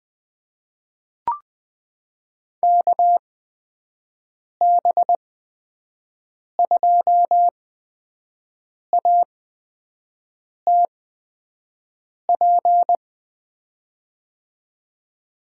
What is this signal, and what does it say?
Morse code, heard as a keyed tone, about 20 wpm, spaced as for 5 wpm, 700 Hz. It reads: KB2ATP
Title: USA Call Signs - 8x Character Spacing - 20wpm